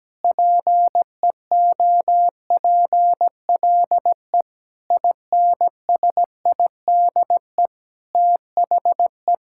Morse code, heard as a keyed tone, about 17 wpm, 700 Hz. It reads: PEOPLE INSIDE THE
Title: Sentences from Top 1000 Words - Review Part 6 - 17wpm